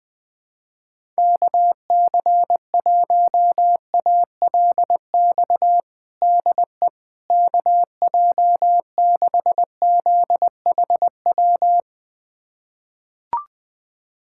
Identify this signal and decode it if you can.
Morse code, heard as a keyed tone, about 20 wpm, 700 Hz. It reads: KC1ALX DE KJ6ZHW
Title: USA Callsign DE Callsign - 6 Characters - 20wpm